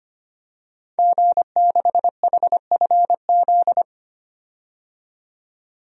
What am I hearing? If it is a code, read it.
Morse code, heard as a keyed tone, about 25 wpm, 700 Hz. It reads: G6HFZ
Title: DXCC-100 Call Signs - 5 Characters - 25wpm